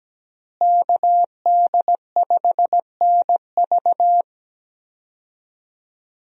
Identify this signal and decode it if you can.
Morse code, heard as a keyed tone, about 17 wpm, 700 Hz. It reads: KD5NV
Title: USA Call Signs - 5 Characters - Repeated 3x - 17wpm